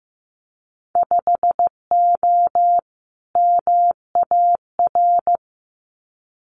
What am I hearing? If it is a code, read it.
Morse code, heard as a keyed tone, about 15 wpm, 700 Hz. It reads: 5O MAR
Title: Winter Field Day Exchanges Encore 15wpm